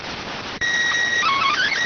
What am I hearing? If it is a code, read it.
SSTV (slow-scan television), PD 290.